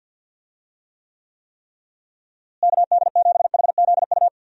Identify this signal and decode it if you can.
Morse code, heard as a keyed tone, about 52 wpm, 700 Hz. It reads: KD7HZU
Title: USA Call Signs - 6 Characters - Repeated 3x - 50wpm